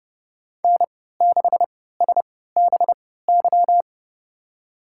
Morse code, heard as a keyed tone, about 31 wpm, spaced as for 17 wpm, 700 Hz. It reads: N6SBY
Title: USA Call Signs - 5 Characters - 3x Character Spacing - 30wpm